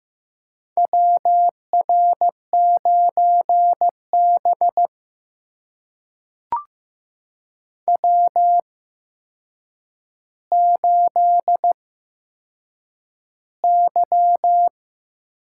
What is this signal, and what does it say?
Morse code, heard as a keyed tone, about 15 wpm, spaced as for 7 wpm, 700 Hz. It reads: WR9B W8Y
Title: USA Call Signs - 4 Characters - 8x Character Spacing - 15wpm